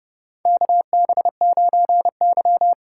Morse code, heard as a keyed tone, about 29 wpm, 700 Hz. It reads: KB9Y
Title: USA Call Signs - 4 Characters - 2x Character Spacing - 30wpm